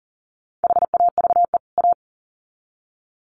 Morse code, heard as a keyed tone, about 40 wpm, 700 Hz. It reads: HAVE A
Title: Sets of 2 Words 40wpm